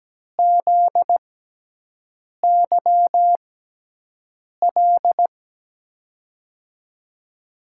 Morse code, heard as a keyed tone, about 17 wpm, spaced as for 6 wpm, 700 Hz. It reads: ZYL
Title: USA Call Signs - 6x Character Spacing - 17wpm